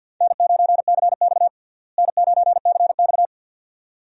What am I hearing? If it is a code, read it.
Morse code, heard as a keyed tone, about 51 wpm, 700 Hz. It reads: N9CX N9CX
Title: USA Call Signs - 4 Characters - Repeated 2x - 50wpm